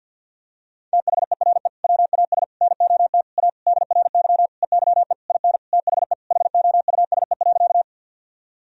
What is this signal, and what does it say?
Morse code, heard as a keyed tone, about 50 wpm, 700 Hz. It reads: THERE WAS NOT A DRY EYE IN THE HOUSE.
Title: Sentences from Top 1000 Words - Review Part 3 - 50wpm